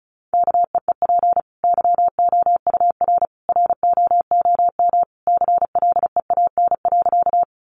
Morse code, heard as a keyed tone, about 34 wpm, 700 Hz. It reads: KEEP YOUR ROOM CLEAN.
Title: Sentences from Top 1100 Words 35wpm